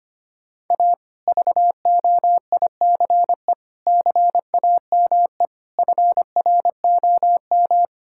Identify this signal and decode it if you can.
Morse code, heard as a keyed tone, about 25 wpm, 700 Hz. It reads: A VOICE CAME FROM
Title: Sentences from Top 1000 Words - Review Part 6 - 25wpm